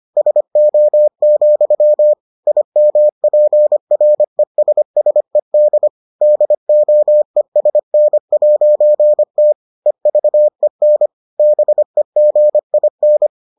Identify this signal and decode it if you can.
Morse code, heard as a keyed tone, about 25 wpm, 600 Hz. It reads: SO, IMPRESSED DOESN'T EVEN BEGIN